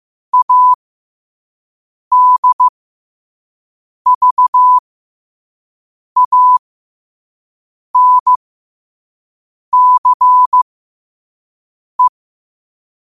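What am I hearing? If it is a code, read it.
Morse code, heard as a keyed tone, about 15 wpm, spaced as for 5 wpm, 1000 Hz. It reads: ADVANCE
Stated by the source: Morse code at 5 wpm.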